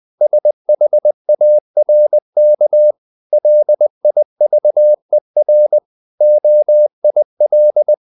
Morse code, heard as a keyed tone, about 20 wpm, 600 Hz. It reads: SHARK LIVER OIL
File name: the old man and the sea / the old man and the sea part 1 20wpm